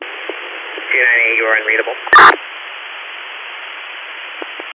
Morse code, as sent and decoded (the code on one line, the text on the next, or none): none